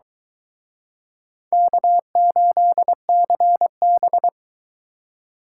Morse code, heard as a keyed tone, about 24 wpm, 700 Hz. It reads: K8CB